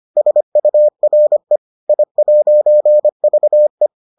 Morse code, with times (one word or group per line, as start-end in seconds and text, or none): 0.2-1.6 SURE
1.9-3.9 I'VE